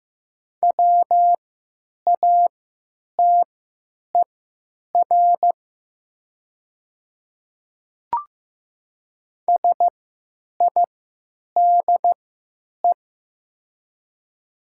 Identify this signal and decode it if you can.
Morse code, heard as a keyed tone, about 15 wpm, spaced as for 9 wpm, 690 Hz. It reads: WATER SIDE
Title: 3x Character Spacing - Top 100 Words - 15wpm